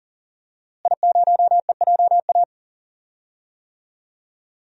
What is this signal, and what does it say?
Morse code, heard as a keyed tone, about 39 wpm, 700 Hz. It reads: I0EJA
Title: DXCC-100 Call Signs - 5 Characters - Repeated 3x - 40wpm